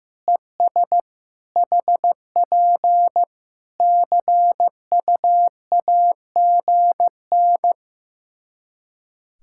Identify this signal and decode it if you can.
Morse code, heard as a keyed tone, about 15 wpm, 700 Hz. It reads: ES HP CUAGN
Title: Practice QSO - Exchange 4 - Ending - 15wpm